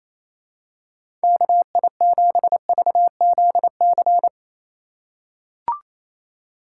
Morse code, heard as a keyed tone, about 28 wpm, 700 Hz. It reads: KI7VZC
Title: USA Call Signs - 6 Characters - 2x Character Spacing - 28wpm